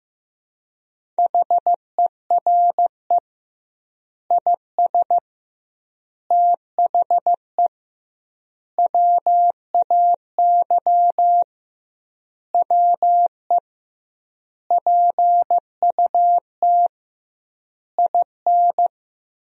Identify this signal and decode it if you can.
Morse code, heard as a keyed tone, about 15 wpm, 700 Hz. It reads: HERE IS THE WAY WE PUT IN
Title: 2x Word-Spacing - Sentences from Top 200 Words - 15wpm